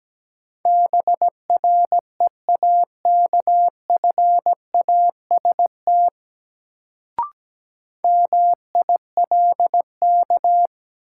Morse code, heard as a keyed tone, about 17 wpm, 700 Hz. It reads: BREAKFAST MILK
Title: Sentences from Top 1500 Words - New Word Drill - 17wpm